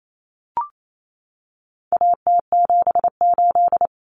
Morse code, heard as a keyed tone, about 28 wpm, 700 Hz. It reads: AT78
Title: Common Cut Numbers - 4-Digit 28wpm